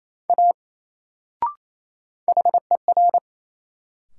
Morse code, heard as a keyed tone, about 28 wpm, 700 Hz. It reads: A HER
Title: Sentences from Top 100 Words - New Word Drill - 28wpm